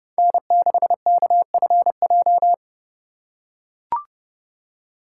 Morse code, heard as a keyed tone, about 30 wpm, 700 Hz. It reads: N6KFJ
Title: USA Call Signs - 5 Characters - 3x Character Spacing - 30wpm